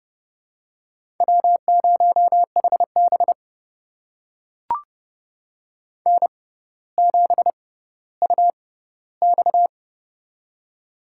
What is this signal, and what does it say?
Morse code, heard as a keyed tone, about 30 wpm, spaced as for 15 wpm, 700 Hz. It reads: W0HB N7UX
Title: USA Call Signs - 4 Characters - 6x Character Spacing - 30wpm